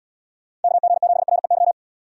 Morse code, heard as a keyed tone, about 76 wpm, 700 Hz. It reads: KG7REY